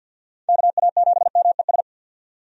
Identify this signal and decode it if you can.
Morse code, heard as a keyed tone, about 51 wpm, 700 Hz. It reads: KA7MES
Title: USA Call Signs - 6 Characters - 50wpm